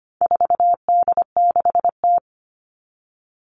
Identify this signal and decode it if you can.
Morse code, heard as a keyed tone, about 25 wpm, 700 Hz. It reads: 4D6T
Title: All Cut Numbers - 4-Digit 25wpm